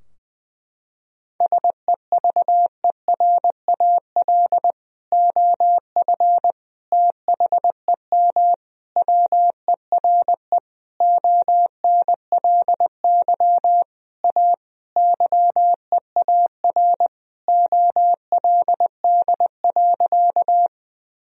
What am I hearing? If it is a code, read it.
Morse code, heard as a keyed tone, about 21 wpm, 700 Hz. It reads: SEVERAL OF THEM WERE ONLY A YEAR OLD.
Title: Sentences from Top 1000 Words - Review Part 2 - 20wpm